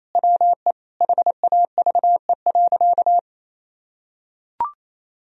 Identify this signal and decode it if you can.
Morse code, heard as a keyed tone, about 28 wpm, 700 Hz. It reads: WE HAVE.
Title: Sentences from Top 1000 Words - Review Part 4 - 28wpm